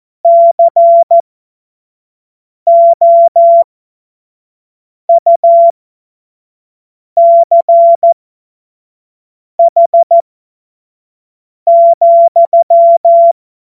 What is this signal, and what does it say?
Morse code, heard as a keyed tone, about 14 wpm, spaced as for 5 wpm, 680 Hz. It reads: COUCH,
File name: Morsekurs / Witze / Witz 05 März 2024 25 F70